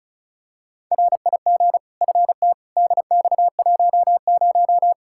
Morse code, heard as a keyed tone, about 35 wpm, 700 Hz. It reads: RIG FT DX10
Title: K1USN SST QSO Practice 35wpm